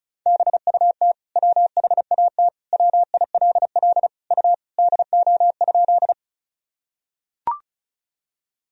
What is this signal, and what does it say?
Morse code, heard as a keyed tone, about 36 wpm, 700 Hz. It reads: BUT WHAT WILL U DO?
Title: Sentences from Top 100 Words and Abbreviations 35wpm